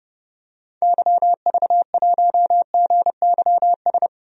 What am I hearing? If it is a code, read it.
Morse code, heard as a keyed tone, about 31 wpm, 690 Hz. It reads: YV1GYS
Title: DXCC-100 Call Signs - 6 Characters - 30wpm